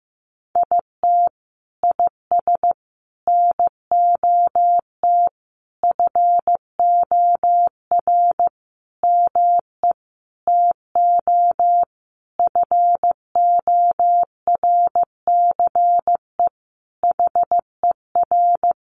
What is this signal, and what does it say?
Morse code, heard as a keyed tone, about 15 wpm, 700 Hz. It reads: IT IS NOT FOR ME TO FORCE HER
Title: Sentences from Top 1500 Words 15wpm